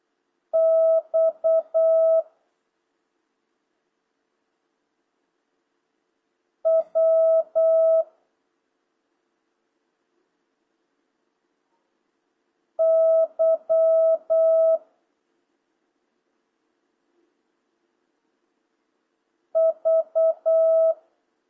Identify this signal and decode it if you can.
Morse code, heard as a keyed tone, about 8 wpm, 650 Hz. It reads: X W Y V